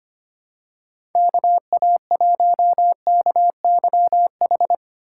Morse code, heard as a keyed tone, about 25 wpm, 700 Hz. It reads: KA1KYH
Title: USA Call Signs - 3x Character Spacing - 25wpm